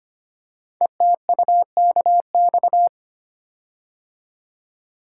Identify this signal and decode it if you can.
Morse code, heard as a keyed tone, about 25 wpm, 700 Hz. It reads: ETUKX
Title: Code Groups - 5 Letters - 25wpm